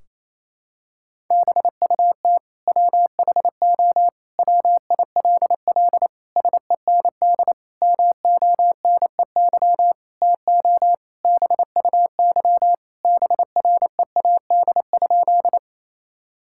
Morse code, heard as a keyed tone, about 28 wpm, 700 Hz. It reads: BUT WHO WILL SEND MONEY TO BUY BREAD?
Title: Sentences from Top 1600 Words 28wpm